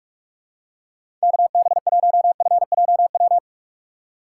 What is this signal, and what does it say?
Morse code, heard as a keyed tone, about 46 wpm, 700 Hz. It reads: KB1FJW
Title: USA Call Signs - 6 Characters - 6x Character Spacing - 45wpm